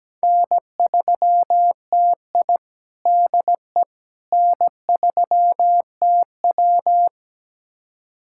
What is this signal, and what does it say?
Morse code, heard as a keyed tone, about 17 wpm, 700 Hz. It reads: N3TI DE N3TW
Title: USA Callsign DE Callsign - 4 Characters - 17wpm